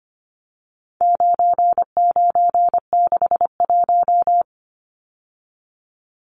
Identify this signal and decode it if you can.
Morse code, heard as a keyed tone, about 25 wpm, 700 Hz. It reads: 9961
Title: Numbers - 4-Digit 25wpm